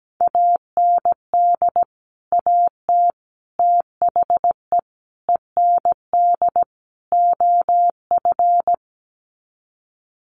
Morse code, heard as a keyed tone, about 17 wpm, 700 Hz. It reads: AND AT THE END OF
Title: Sets of 5 Words 17wpm